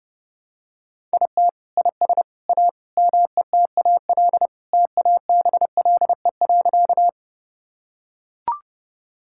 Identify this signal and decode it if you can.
Morse code, heard as a keyed tone, about 30 wpm, 700 Hz. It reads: IT IS A METAL TABLE.
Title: Sentences from Top 1000 Words - Review Part 6 - 30wpm